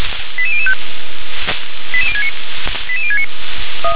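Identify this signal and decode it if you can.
Five-tone selective call (ZVEI), heard as quick runs of five tones, tones of about 70 ms. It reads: 88005 88069 88069
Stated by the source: ZVEI five-tone call received by SDR.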